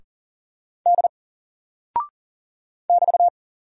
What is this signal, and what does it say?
Morse code, heard as a keyed tone, about 39 wpm, 700 Hz. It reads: D <BT>
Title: Single Character (Letter:Number:Punctuation:Prosign) Alt 40wpm